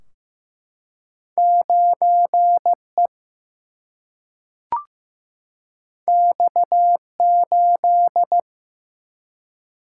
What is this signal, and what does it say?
Morse code, heard as a keyed tone, about 15 wpm, 700 Hz. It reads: 9 E X 8
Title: Single-Letter Single-Number 15wpm